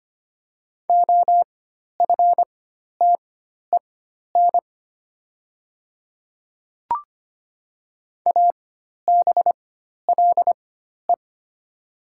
Morse code, heard as a keyed tone, about 25 wpm, spaced as for 12 wpm, 700 Hz. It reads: OFTEN ABLE
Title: 4x Character Spacing - Top 500 Words - 25wpm